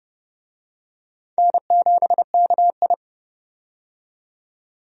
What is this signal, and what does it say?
Morse code, heard as a keyed tone, about 29 wpm, 700 Hz. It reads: N7KI
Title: USA Call Signs - 4 Characters - 30wpm